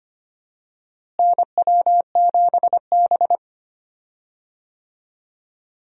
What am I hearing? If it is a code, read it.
Morse code, heard as a keyed tone, about 25 wpm, 700 Hz. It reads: NW7B
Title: USA Call Signs - 4 Characters - Repeated 2x - 25wpm